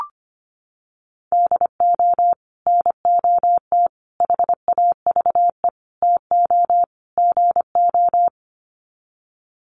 Morse code, heard as a keyed tone, about 25 wpm, 700 Hz. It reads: DO NOT HAVE TO GO
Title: Sets of 5 Words 25wpm